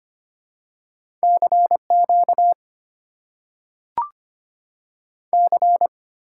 Morse code, heard as a keyed tone, about 25 wpm, 690 Hz. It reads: CQ C
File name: Top 90 QSO Elements - 6x Character Spacing - 25wpm